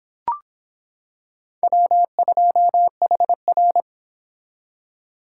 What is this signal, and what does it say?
Morse code, heard as a keyed tone, about 25 wpm, 700 Hz. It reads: W2HR